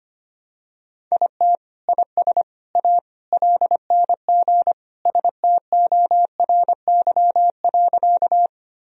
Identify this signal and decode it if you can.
Morse code, heard as a keyed tone, about 25 wpm, 700 Hz. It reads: IT IS A LNG STORY.
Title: Sentences from Top 300 Words and Abbreviations 25wpm